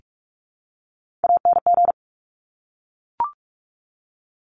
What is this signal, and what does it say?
Morse code, heard as a keyed tone, about 46 wpm, 700 Hz. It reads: ANG
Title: Top Three Letter Combinations in Words 45wpm